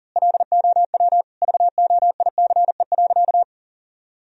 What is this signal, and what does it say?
Morse code, heard as a keyed tone, about 37 wpm, 700 Hz. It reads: LOW VOICE.
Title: Sentences from Top 1000 Words - Review Part 1 - 40wpm